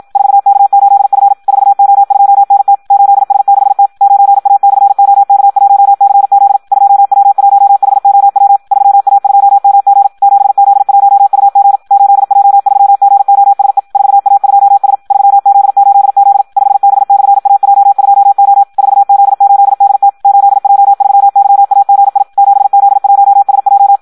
Morse code, heard as a keyed tone, about 56 wpm, 800 Hz. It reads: 4K8F 4O2NT 8S6T 9A/OK1CZ 2W1HOW 3A2MG 7Z1UG 8J3GOSE 4A2S 3Z8Z 5B/AJ2O 4Z8DT 8J3YAGI 7Z1SJ
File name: Audio practica CW / qrq-tx-54 2